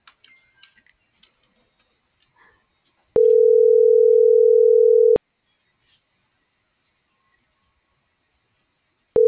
Ringback tone (North America), 440 + 480 Hz.